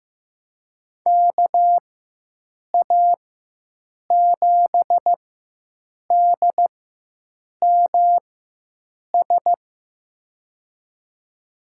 Morse code, heard as a keyed tone, about 15 wpm, spaced as for 7 wpm, 700 Hz. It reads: KA7DMS
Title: USA Call Signs - 6 Characters - 4x Character Spacing - 15wpm